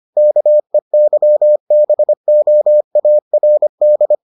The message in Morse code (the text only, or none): KEYBOARD